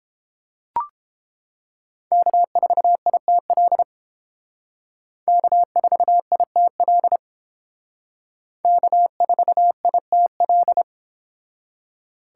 Morse code, heard as a keyed tone, about 30 wpm, 700 Hz. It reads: K4ITL K4ITL K4ITL